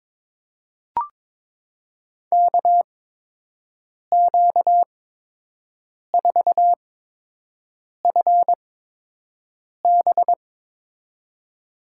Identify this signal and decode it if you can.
Morse code, heard as a keyed tone, about 22 wpm, spaced as for 6 wpm, 700 Hz. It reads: KQ4FB
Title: USA Call Signs - 5 Characters - 8x Character Spacing - 22wpm